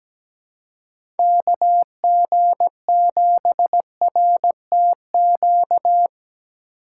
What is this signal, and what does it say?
Morse code, heard as a keyed tone, about 17 wpm, 700 Hz. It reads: KG7RTQ
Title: USA Call Signs - 6 Characters - 3x Character Spacing - 17wpm